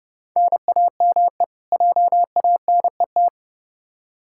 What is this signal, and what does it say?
Morse code, heard as a keyed tone, about 30 wpm, 700 Hz. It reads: NAME JANET
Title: K1USN SST QSO Practice 30wpm